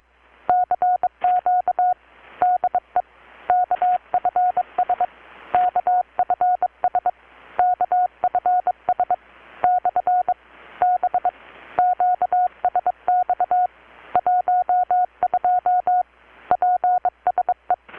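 Morse code, heard as a keyed tone, about 22 wpm, 720 Hz. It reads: CQ DE KFS KFS KFS / B QSX 12 PSE
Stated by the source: Morse from KPH's Night of Nights XX maritime radiotelegraphy event, received in Chiapas, Mexico about 3,800 km away on a Kenwood R-600 with a grounded 10 m long-wire antenna.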